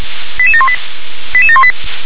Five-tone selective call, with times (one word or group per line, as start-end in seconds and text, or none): 0.4-0.7 88718
1.3-1.7 88718